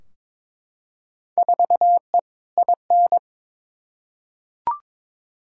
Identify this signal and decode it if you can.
Morse code, heard as a keyed tone, about 22 wpm, 700 Hz. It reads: 4E IN
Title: Field Day Exchanges - 22wpm